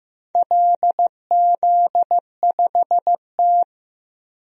Morse code, heard as a keyed tone, about 15 wpm, 700 Hz. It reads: LZ5T
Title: DXCC-100 Call Signs - 4 Characters - 15wpm